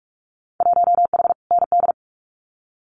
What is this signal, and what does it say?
Morse code, heard as a keyed tone, about 44 wpm, 700 Hz. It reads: 1H ND
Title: Winter Field Day Exchanges 45wpm